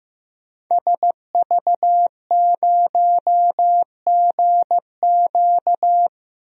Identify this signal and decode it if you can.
Morse code, heard as a keyed tone, about 15 wpm, 700 Hz. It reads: SV0GQ
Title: DXCC-100 Call Signs - 5 Characters - Repeated 3x - 15wpm